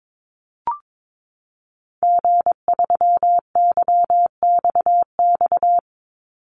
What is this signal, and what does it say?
Morse code, heard as a keyed tone, about 22 wpm, 700 Hz. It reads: G3YXX
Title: Call Signs - 22wpm